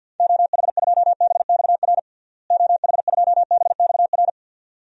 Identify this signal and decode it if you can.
Morse code, heard as a keyed tone, about 45 wpm, 700 Hz. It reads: OH2BXR OH2BXR
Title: DXCC-100 Call Signs - 6 Characters - Repeated 2x - 50wpm